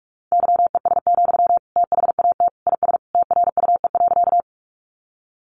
Morse code, heard as a keyed tone, about 44 wpm, 700 Hz. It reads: YES, THAT IS TRUE.
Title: Sentences from Top 500 Words 45wpm